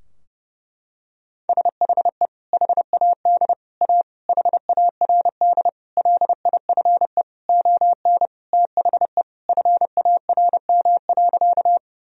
Morse code, heard as a keyed tone, about 30 wpm, 700 Hz. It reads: SHE HAD A HARD LIFE ON THE FARM.
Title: Sentences from Top 1000 Words - Review Part 2 - 30wpm